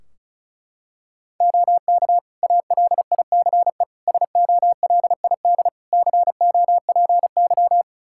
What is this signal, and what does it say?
Morse code, heard as a keyed tone, about 36 wpm, 700 Hz. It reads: OK ALICE SOLID COPY
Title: Practice QSO - Exchange 3 - More Info - 35wpm